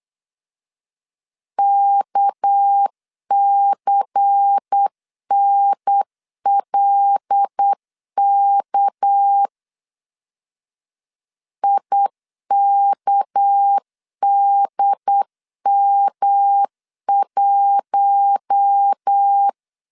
Morse code, heard as a keyed tone, about 8 wpm, 800 Hz. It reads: KCNLK IKDM1
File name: LEZIONI / 6 PKX0 / 6 PK 8wpm 3003